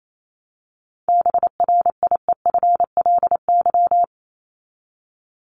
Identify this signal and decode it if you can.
Morse code, heard as a keyed tone, about 28 wpm, 700 Hz. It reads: BRIEFLY
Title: LY Suffix - Top 100 Words with Base Words 28wpm